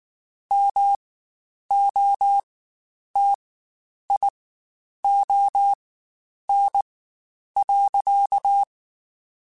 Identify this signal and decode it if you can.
Morse code, heard as a keyed tone, about 19 wpm, spaced as for 9 wpm, 800 Hz. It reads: MOTION.
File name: Spaced / Chuckles Spaced-mp3s / BrainMouth sp @18